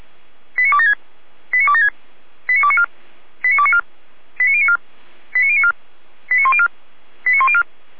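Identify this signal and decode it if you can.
Five-tone selective call (ZVEI), heard as quick runs of five tones, tones of about 70 ms. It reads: 89276 89276 89294 89294 89094 89094 89104 89104
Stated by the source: ZVEI five-tone selective call 89265 received via SDR.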